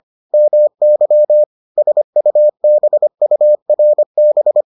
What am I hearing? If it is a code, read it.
Morse code, heard as a keyed tone, about 25 wpm, 600 Hz. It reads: MY SUBURB